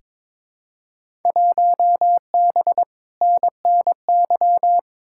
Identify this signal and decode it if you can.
Morse code, heard as a keyed tone, about 22 wpm, 700 Hz. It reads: 1B NNY
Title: Field Day Exchanges Encore - 22wpm